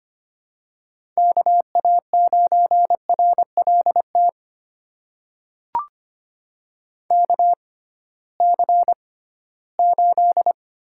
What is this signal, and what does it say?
Morse code, heard as a keyed tone, about 25 wpm, spaced as for 16 wpm, 700 Hz. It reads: KA9RLT KC8
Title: USA Call Signs - 6 Characters - 6x Character Spacing - 25wpm